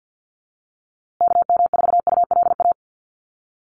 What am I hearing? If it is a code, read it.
Morse code, heard as a keyed tone, about 49 wpm, 700 Hz. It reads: KM4ULA
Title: USA Call Signs Repeated 2x 50wpm